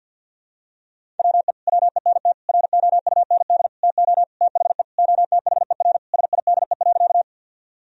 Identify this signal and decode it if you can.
Morse code, heard as a keyed tone, about 47 wpm, 700 Hz. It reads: WE WENT ROUND TO THE OTHER SIDE.
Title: Sentences from Top 1000 Words - Review Part 1 - 50wpm